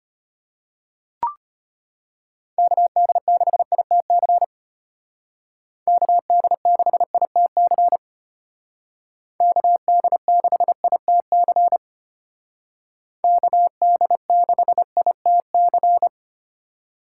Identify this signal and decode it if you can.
Morse code, heard as a keyed tone, about 33 wpm, 700 Hz. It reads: KD6ITC KD6ITC KD6ITC KD6ITC